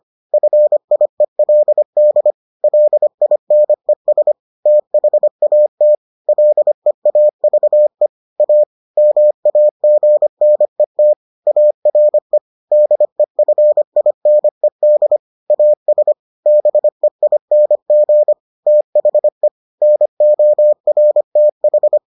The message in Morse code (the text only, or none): FIELD LINES THAT LEAVE A MAGNET ARE DEFINED AS BEING THE NORTH